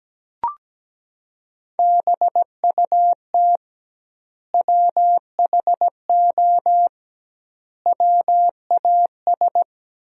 Morse code, heard as a keyed tone, about 17 wpm, 700 Hz. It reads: BUT WHO WAS
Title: 2x Word-Spacing - Sentences from Top 200 Words - 17wpm